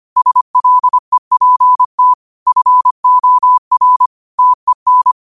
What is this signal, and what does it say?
Morse code, heard as a keyed tone, about 25 wpm, 1000 Hz. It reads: SLEPT FOR TEN
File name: qotdfiles / 25wpm